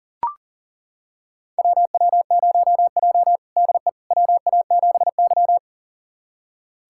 Morse code, heard as a keyed tone, about 40 wpm, 700 Hz. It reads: WW0J DE WA7Y
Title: USA Callsign DE Callsign - 4 Characters - 40wpm